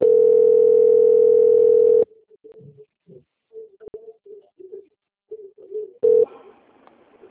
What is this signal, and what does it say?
Ringback tone (North America), 440 + 480 Hz.